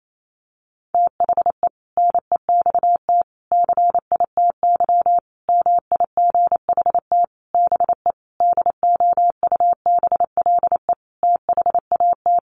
Morse code, heard as a keyed tone, about 28 wpm, 700 Hz. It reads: THE NEXT CITY MIGHT BE DOUBLE THAT
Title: Sentences from Top 1400 Words 28wpm